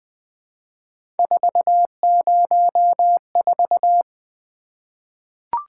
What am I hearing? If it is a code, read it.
Morse code, heard as a keyed tone, about 20 wpm, 700 Hz. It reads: 404
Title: Three Letter-Number Combinations - 20wpm